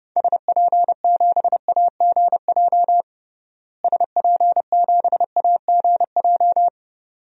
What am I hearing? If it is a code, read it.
Morse code, heard as a keyed tone, about 32 wpm, 700 Hz. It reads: SP7AGJ SP7AGJ
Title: DXCC-100 Call Signs - 6 Characters - Repeated 2x - 30wpm